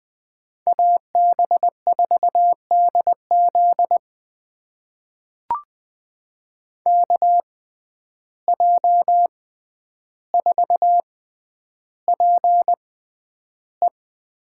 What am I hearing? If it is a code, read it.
Morse code, heard as a keyed tone, about 20 wpm, spaced as for 10 wpm, 700 Hz. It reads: AB4DZ KJ4PE
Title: USA Call Signs - 5 Characters - 6x Character Spacing - 20wpm